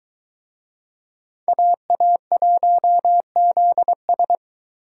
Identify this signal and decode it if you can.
Morse code, heard as a keyed tone, about 22 wpm, 700 Hz. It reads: AA1ZS